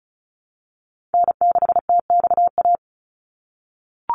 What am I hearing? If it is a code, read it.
Morse code, heard as a keyed tone, about 34 wpm, 700 Hz. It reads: N6TXA
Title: USA Call Signs 35wpm